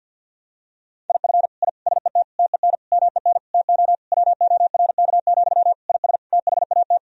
Morse code, heard as a keyed tone, about 50 wpm, 700 Hz. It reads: IF I SET TEN MEN TO WORK, IS THAT
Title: Sentences from Top 1000 Words - Review Part 2 - 50wpm